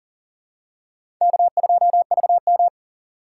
Morse code, heard as a keyed tone, about 40 wpm, 700 Hz. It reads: K2VM
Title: USA Call Signs - 4 Characters - 6x Character Spacing - 40wpm